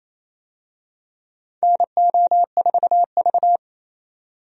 Morse code, heard as a keyed tone, about 28 wpm, 690 Hz. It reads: NO4V
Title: USA Call Signs - 4 Characters - 28wpm